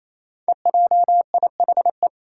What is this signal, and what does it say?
Morse code, heard as a keyed tone, about 28 wpm, 700 Hz. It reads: EJIHE
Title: Five Challenging Letter-Number Combinations 28wpm